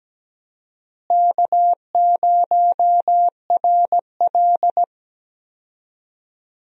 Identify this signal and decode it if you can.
Morse code, heard as a keyed tone, about 17 wpm, 700 Hz. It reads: K0RL
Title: USA Call Signs - 4 Characters - Repeated 2x - 17wpm